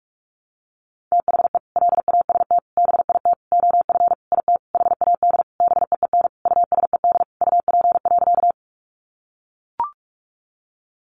Morse code, heard as a keyed tone, about 44 wpm, 700 Hz. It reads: THE LAST BIT OF IT HAD BEEN USED UP.
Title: Sentences from Top 900 Words 45wpm